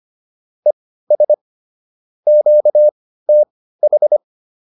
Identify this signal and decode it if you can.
Morse code, heard as a keyed tone, about 25 wpm, spaced as for 15 wpm, 600 Hz. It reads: ES QTH
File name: QSO402 15